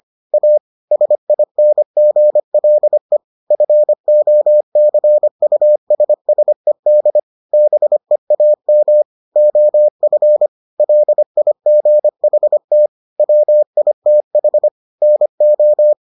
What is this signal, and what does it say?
Morse code, heard as a keyed tone, about 25 wpm, 600 Hz. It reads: A SINGLE FOCUSSED BEAM OF LIGHT WITH NO